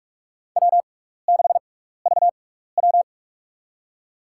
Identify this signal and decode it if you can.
Morse code, heard as a keyed tone, about 46 wpm, spaced as for 15 wpm, 700 Hz. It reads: W6VW
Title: USA Call Signs - 6x Character Spacing - 45wpm